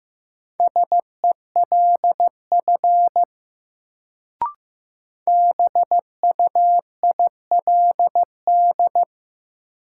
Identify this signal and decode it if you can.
Morse code, heard as a keyed tone, about 15 wpm, 700 Hz. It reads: SELF BUILD
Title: Sentences from Top 1000 Words - Review Part 1 - New Word Drill - 15wpm